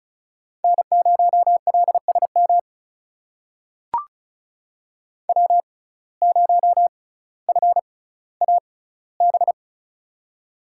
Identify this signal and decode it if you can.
Morse code, heard as a keyed tone, about 36 wpm, spaced as for 18 wpm, 700 Hz. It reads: N0LSM W0FAB
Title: USA Call Signs - 5 Characters - 6x Character Spacing - 35wpm